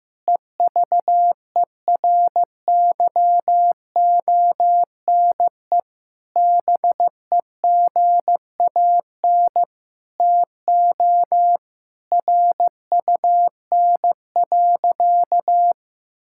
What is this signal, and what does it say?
Morse code, heard as a keyed tone, about 15 wpm, 700 Hz. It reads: EVERYONE BEGAN TO RUN.
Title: Sentences from Top 1600 Words 15wpm